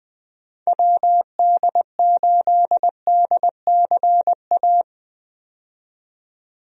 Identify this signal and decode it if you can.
Morse code, heard as a keyed tone, about 21 wpm, 700 Hz. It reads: WD8DCA
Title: USA Call Signs - 6 Characters - 20wpm